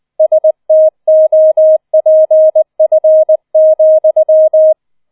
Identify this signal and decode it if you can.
Morse code, heard as a keyed tone, about 20 wpm, 630 Hz. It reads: STOPF,